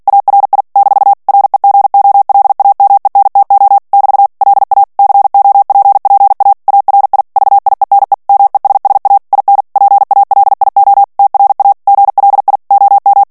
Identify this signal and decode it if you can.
Morse code, heard as a keyed tone, about 48 wpm, 800 Hz. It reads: ARI - REGOLAMENTO <BT> LA COPPA ARI VIENE MESSA IN PALIO TRA GLI OM
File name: ITALIEN texte clair 2 vitesse 45 mots minute